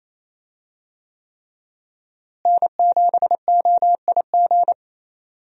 Morse code, heard as a keyed tone, about 28 wpm, 700 Hz. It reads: N7OIG
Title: USA Call Signs - 5 Characters - Repeated 3x - 28wpm